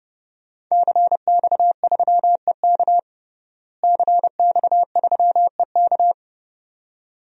Morse code, heard as a keyed tone, about 32 wpm, 700 Hz. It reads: CX3EK CX3EK
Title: DXCC-100 Call Signs - 5 Characters - Repeated 2x - 30wpm